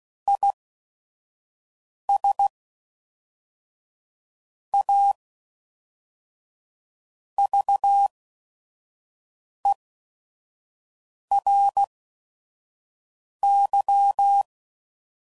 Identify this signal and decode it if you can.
Morse code, heard as a keyed tone, about 16 wpm, spaced as for 5 wpm, 800 Hz. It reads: IS A VERY
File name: CW Code Groups @5WPM / Technical Subjects / Reactance